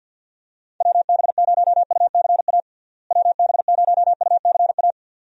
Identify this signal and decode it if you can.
Morse code, heard as a keyed tone, about 52 wpm, 700 Hz. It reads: WB0UCA WB0UCA
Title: USA Call Signs - 6 Characters - Repeated 2x - 50wpm